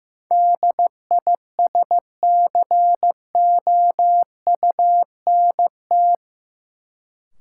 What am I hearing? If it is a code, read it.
Morse code, heard as a keyed tone, about 15 wpm, 700 Hz. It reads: DISCOUNT
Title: Sentences from Top 1200 Words - New Word Drill Alt - 15wpm